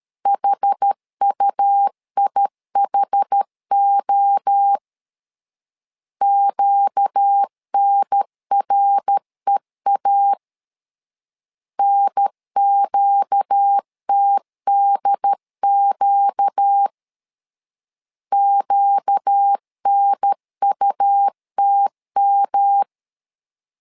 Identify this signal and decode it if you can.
Morse code, heard as a keyed tone, about 13 wpm, 800 Hz. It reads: HUIHO QNREA NQTDQ QNUTM
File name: LEZIONI / 4 Q 12wpm 1833